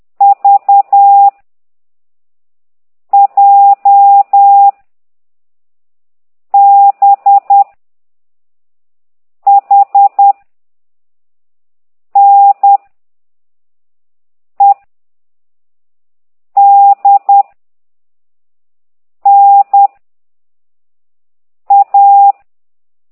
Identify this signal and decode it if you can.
Morse code, heard as a keyed tone, about 10 wpm, 800 Hz. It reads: V J B H N E D N A
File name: exerc7